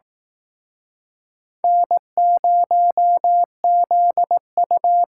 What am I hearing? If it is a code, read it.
Morse code, heard as a keyed tone, about 19 wpm, 700 Hz. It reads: N0ZU